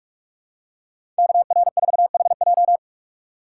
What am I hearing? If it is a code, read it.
Morse code, heard as a keyed tone, about 42 wpm, 700 Hz. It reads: KR4HJ
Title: USA Call Signs - 5 Characters - 3x Character Spacing - 45wpm